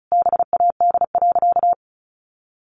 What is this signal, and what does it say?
Morse code, heard as a keyed tone, about 34 wpm, 700 Hz. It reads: BAD.
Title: Additional Easy Phrases 35wpm